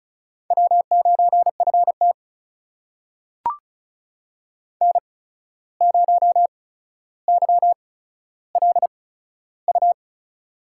Morse code, heard as a keyed tone, about 36 wpm, spaced as for 14 wpm, 700 Hz. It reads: W9FT N0YLU
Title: USA Call Signs - 8x Character Spacing - 35wpm